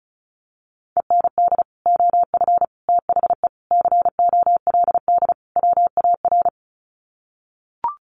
Morse code, heard as a keyed tone, about 34 wpm, 700 Hz. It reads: END OF THE COLD WAR
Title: Sets of 5 Words Encore 35wpm